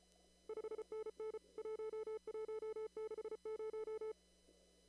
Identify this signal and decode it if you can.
Morse code, heard as a keyed tone, about 34 wpm, 440 Hz. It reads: 5NN 1160
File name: CW / kt5j wpx cw-op-k5pi